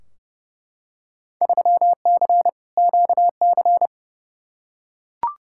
Morse code, heard as a keyed tone, about 30 wpm, 700 Hz. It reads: 3C QC
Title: Field Day Exchanges Encore - 30wpm